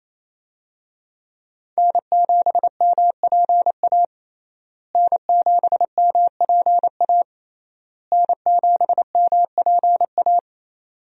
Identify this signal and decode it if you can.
Morse code, heard as a keyed tone, about 27 wpm, 700 Hz. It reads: N7MPA N7MPA N7MPA
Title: USA Call Signs - 5 Characters - Repeated 3x - 28wpm